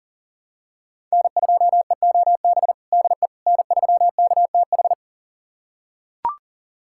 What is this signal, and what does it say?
Morse code, heard as a keyed tone, about 40 wpm, 700 Hz. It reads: N2EOB DE N3KTH
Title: USA Callsign DE Callsign - 5 Characters - 40wpm